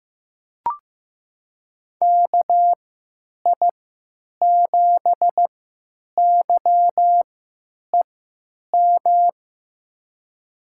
Morse code, heard as a keyed tone, about 15 wpm, spaced as for 9 wpm, 700 Hz. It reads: KI7YEM
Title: USA Call Signs - 6 Characters - 3x Character Spacing - 15wpm